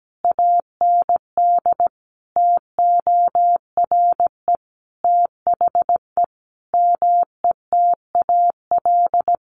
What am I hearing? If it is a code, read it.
Morse code, heard as a keyed tone, about 17 wpm, 700 Hz. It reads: AND TORE THE METAL
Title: A Sound Of Thunder 17wpm